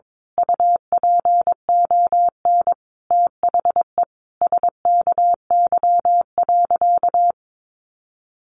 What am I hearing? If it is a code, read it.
Morse code, heard as a keyed tone, about 22 wpm, 700 Hz. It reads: UPON THE SKY.
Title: A Sound Of Thunder 22wpm